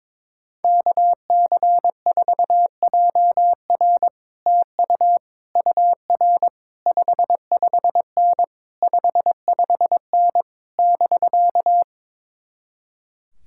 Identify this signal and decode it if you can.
Morse code, heard as a keyed tone, about 22 wpm, 700 Hz. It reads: KC4JR TU UR 55N 55N <BK>
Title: Speed-Racing Common QSO Phrases - 22wpm